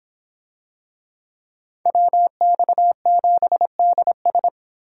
Morse code, heard as a keyed tone, about 26 wpm, 700 Hz. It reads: WX7DS